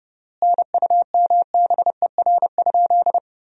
Morse code, heard as a keyed tone, about 32 wpm, 700 Hz. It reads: NUMBER?
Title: 4x Word-Spacing - Sentences from Top 100 Words - 30wpm